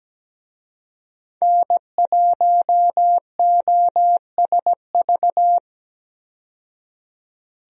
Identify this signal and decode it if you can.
Morse code, heard as a keyed tone, about 17 wpm, 700 Hz. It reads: N1OSV
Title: USA Call Signs - 5 Characters - 17wpm